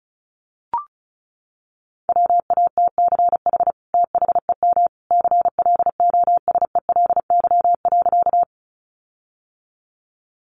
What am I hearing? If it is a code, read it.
Morse code, heard as a keyed tone, about 34 wpm, 700 Hz. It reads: WATCH THEM CLOSELY.
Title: Sentences from Top 1400 Words 35wpm